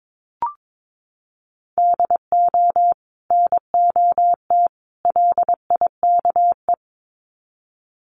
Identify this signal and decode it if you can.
Morse code, heard as a keyed tone, about 22 wpm, 700 Hz. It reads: DO NOT LIKE
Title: Sets of 3 Words 22wpm